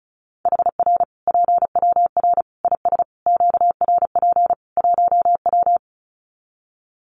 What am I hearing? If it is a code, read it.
Morse code, heard as a keyed tone, about 34 wpm, 700 Hz. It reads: HR PWR IS QRP 1W
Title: Common QSO Phrases 35wpm